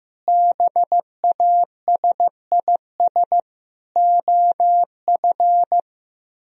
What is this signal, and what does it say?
Morse code, heard as a keyed tone, about 15 wpm, 700 Hz. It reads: BASIS OF
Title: Speed-Racing - Sets of 4 Words 15wpm